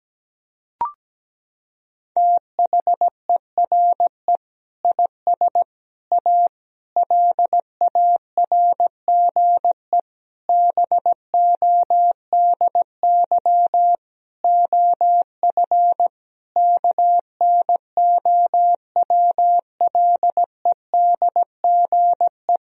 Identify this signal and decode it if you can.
Morse code, heard as a keyed tone, about 17 wpm, 700 Hz. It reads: THERE IS A LARGE BODY OF KNOWLEDGE
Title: Sentences from Top 1000 Words - Review Part 4 - 17wpm